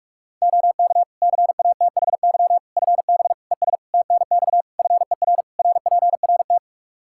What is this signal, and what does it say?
Morse code, heard as a keyed tone, about 46 wpm, 700 Hz. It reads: OK CATHY FB ES TNX FER RPRT
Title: Practice QSO - Exchange 2 - Info - 45wpm